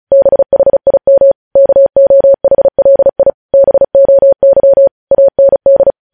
Morse code, heard as a keyed tone, about 36 wpm, 550 Hz. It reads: BHIM KOHLI BOY AND
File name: podcast / News headlines-1x-35x35